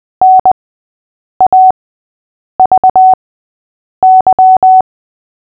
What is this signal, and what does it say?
Morse code, heard as a keyed tone, about 20 wpm, spaced as for 8 wpm, 750 Hz. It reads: NAVY
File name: cw / fists / Short-Story-3-8wpm